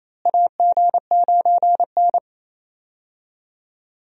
Morse code, heard as a keyed tone, about 28 wpm, 700 Hz. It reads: AG9N
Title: USA Call Signs - 4 Characters - Repeated 3x - 28wpm